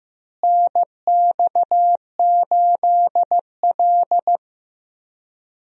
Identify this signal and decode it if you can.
Morse code, heard as a keyed tone, about 15 wpm, 700 Hz. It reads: NX8L
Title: USA Call Signs - 4 Characters - Repeated 2x - 15wpm